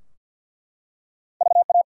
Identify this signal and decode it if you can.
Morse code, heard as a keyed tone, about 50 wpm, 700 Hz. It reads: VA
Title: US States and Canadian Provinces 50wpm